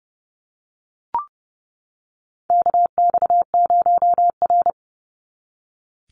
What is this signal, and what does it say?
Morse code, heard as a keyed tone, about 30 wpm, 700 Hz. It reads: KX0R
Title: NA SOTA Activator Callsigns 30wpm